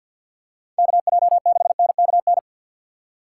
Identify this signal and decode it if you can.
Morse code, heard as a keyed tone, about 53 wpm, 700 Hz. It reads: KJ6NKN
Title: USA Call Signs - 2x Character Spacing - 50wpm